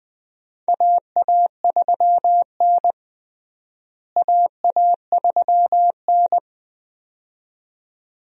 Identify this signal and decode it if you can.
Morse code, heard as a keyed tone, about 20 wpm, 700 Hz. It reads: AA3N AA3N
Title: USA Call Signs - 4 Characters - Repeated 2x - 20wpm